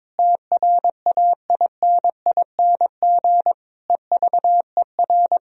Morse code, heard as a keyed tone, about 21 wpm, 700 Hz. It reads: TRAINING EVER
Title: Sentences from Top 1000 Words - Review Part 4 - 22wpm